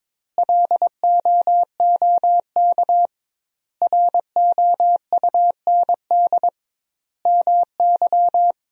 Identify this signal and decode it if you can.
Morse code, heard as a keyed tone, about 22 wpm, 700 Hz. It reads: LOOK ROUND MY
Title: 2x Word-Spacing - Sentences from Top 200 Words - 22wpm